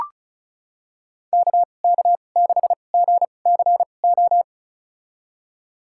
Morse code, heard as a keyed tone, about 36 wpm, spaced as for 25 wpm, 700 Hz. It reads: KK6GCO
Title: USA Call Signs - 6 Characters - 2x Character Spacing - 35wpm